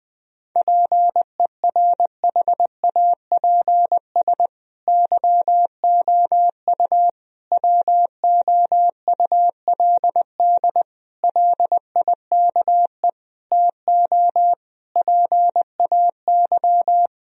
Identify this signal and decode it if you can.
Morse code, heard as a keyed tone, about 20 wpm, 700 Hz. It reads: PERHAPS YOU WOULD LIKE TO PAY
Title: Sentences from Top 1000 Words - Review Part 2 - 20wpm